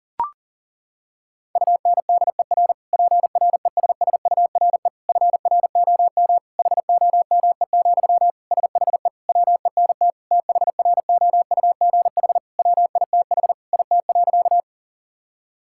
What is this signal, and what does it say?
Morse code, heard as a keyed tone, about 40 wpm, 700 Hz. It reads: UNDER PRESSURE FROM HOME, SHE WENT THROUGH WITH IT.
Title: Sentences from Top 1000 Words - Review Part 6 - 40wpm